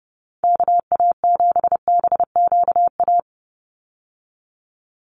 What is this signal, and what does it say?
Morse code, heard as a keyed tone, about 30 wpm, 700 Hz. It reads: KA7BQA
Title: USA Call Signs 30wpm